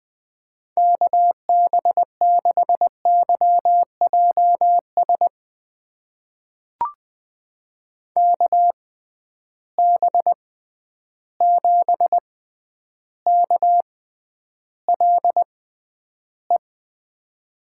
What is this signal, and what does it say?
Morse code, heard as a keyed tone, about 20 wpm, spaced as for 10 wpm, 700 Hz. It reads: KB6YJS KB7KLE
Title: USA Call Signs - 6 Characters - 6x Character Spacing - 20wpm